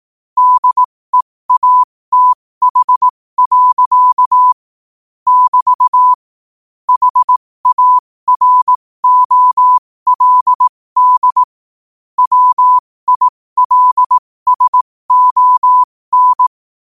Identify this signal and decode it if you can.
Morse code, heard as a keyed tone, about 18 wpm, spaced as for 15 wpm, 1000 Hz. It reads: DEATH. <BT> HAROLD WILSON
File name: qotdfiles / 15wpm